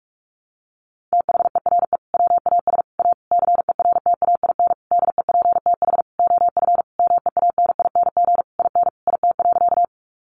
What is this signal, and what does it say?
Morse code, heard as a keyed tone, about 44 wpm, 700 Hz. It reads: THERE WAS A CERTAIN DEPTH OF MEANING IN IT.
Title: Sentences from Top 1200 Words 45wpm